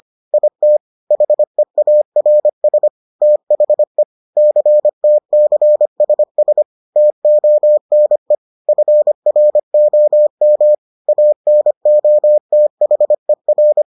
Morse code, heard as a keyed tone, about 25 wpm, 600 Hz. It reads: IT HEARS THE CTCSS TONE FROM ANOTHER